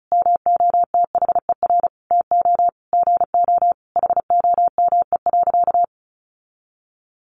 Morse code, heard as a keyed tone, about 34 wpm, 700 Hz. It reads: MOTHER TO GO HOME.
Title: Sentences from Top 1400 Words 35wpm